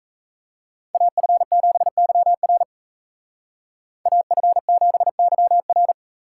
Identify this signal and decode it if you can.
Morse code, heard as a keyed tone, about 33 wpm, 700 Hz. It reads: AF7YR AF7YR
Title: Speed Racing - USA Call Signs 28wpm